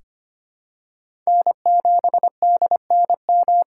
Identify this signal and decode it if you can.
Morse code, heard as a keyed tone, about 25 wpm, 700 Hz. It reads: N7DNM
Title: USA Call Signs - 5 Characters - 3x Character Spacing - 25wpm